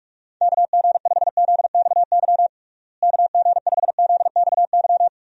Morse code, heard as a keyed tone, about 41 wpm, 700 Hz. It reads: KG5ZXY KG5ZXY
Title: USA Call Signs - 6 Characters - Repeated 2x - 45wpm